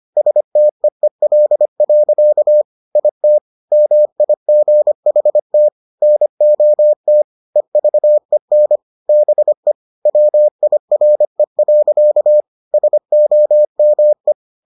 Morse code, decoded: STEEL. IT MIGHT NOT EVEN BE WIRE. SOME

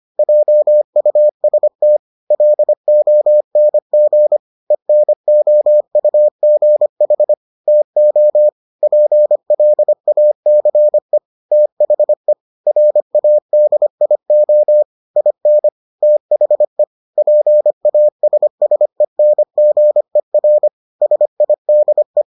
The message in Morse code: JUST LONG ENOUGH TO PLACE THE RADIO IN THE PASSENGER SIDE

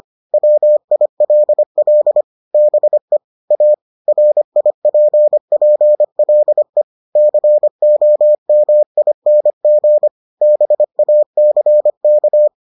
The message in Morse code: WILL BE A RIPPLE COMING BACK

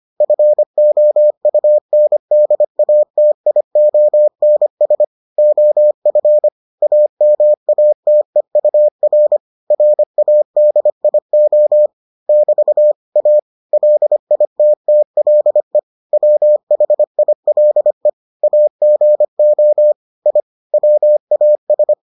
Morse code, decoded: FOUNDATIONS OF AMATEUR RADIO <BT> A LITTLE WHILE AGO I WAS

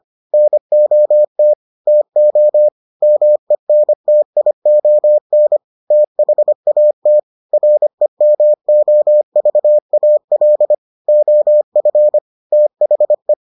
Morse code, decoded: NOT TO MENTION THAT REMOVAL OF THE

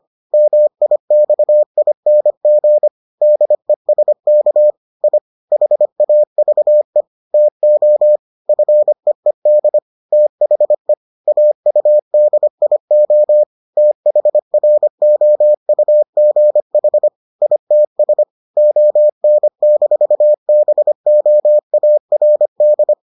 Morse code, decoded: MIXING DESK I HAVE TO FEED THE AUDIO THROUGH ITS ON-BOARD

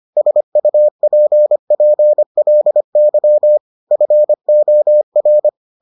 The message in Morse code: SUPPLY FOR

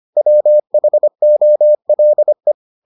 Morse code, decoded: WHOLE

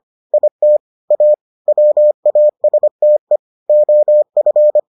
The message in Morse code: IT A WASTE OF

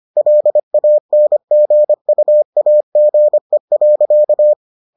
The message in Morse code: LANGUAGE.

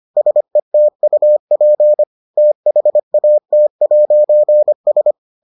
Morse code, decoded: SETUP THAT'S

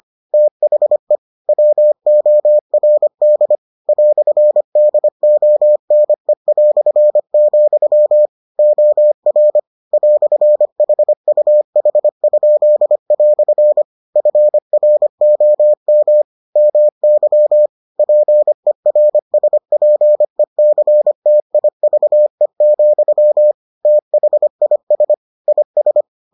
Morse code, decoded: THE WORD "DONE", OR "HUH?" FROM MY PERSPECTIVE, THIS IS

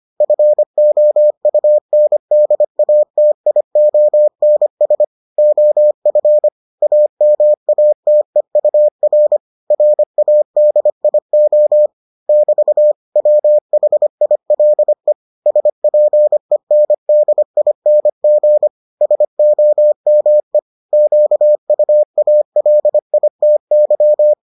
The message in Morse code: FOUNDATIONS OF AMATEUR RADIO <BT> WHILE SPENDING SOME QUALITY

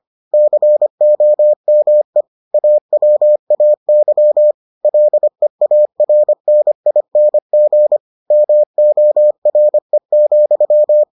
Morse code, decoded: COME AWAY LEARNING MORE,

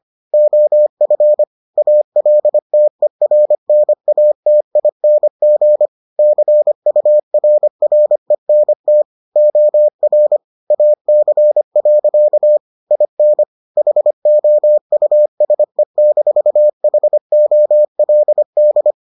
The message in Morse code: OF ALTERNATING CURRENT OR AC. IN HOUSE-HOLD

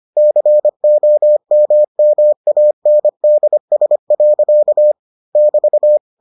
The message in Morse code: COMMANDS. <BT>

